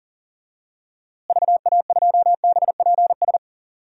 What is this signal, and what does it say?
Morse code, heard as a keyed tone, about 40 wpm, 700 Hz. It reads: VA2BPS
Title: DXCC-100 Call Signs - 6 Characters - Repeated 2x - 40wpm